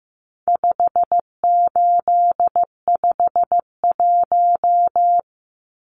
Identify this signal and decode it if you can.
Morse code, heard as a keyed tone, about 15 wpm, 700 Hz. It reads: 5851
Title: Numbers - 4-Digit 15wpm